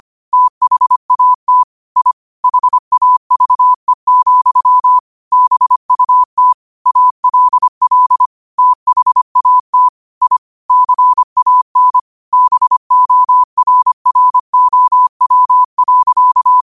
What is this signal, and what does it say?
Morse code, heard as a keyed tone, about 25 wpm, 1000 Hz. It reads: THAT I HAVE, BUT ALL THAT I CAN BORROW.